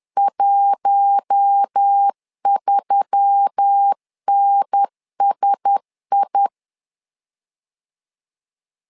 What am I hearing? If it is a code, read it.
Morse code, heard as a keyed tone, about 10 wpm, 800 Hz. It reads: 13NSI
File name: LEZIONI / 5 LBVC / 5 LBVC 10wpm 2349